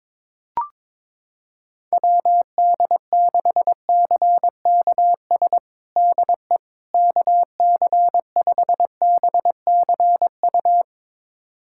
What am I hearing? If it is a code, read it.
Morse code, heard as a keyed tone, about 22 wpm, 700 Hz. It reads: WD6CKS DE KC5BCU
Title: USA Callsign DE Callsign - 6 Characters - 22wpm